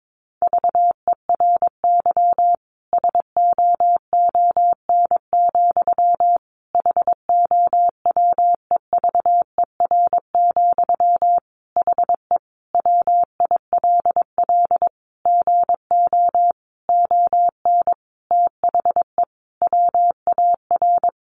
Morse code, heard as a keyed tone, about 22 wpm, 700 Hz. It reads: VERY SOON, HOWEVER, HE WILL GO ON THE WAR
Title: Sentences from Top 1400 Words 22wpm